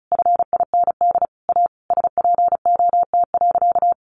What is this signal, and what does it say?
Morse code, heard as a keyed tone, about 34 wpm, 700 Hz. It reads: FIND A SPOT.
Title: Sentences from Top 1100 Words 35wpm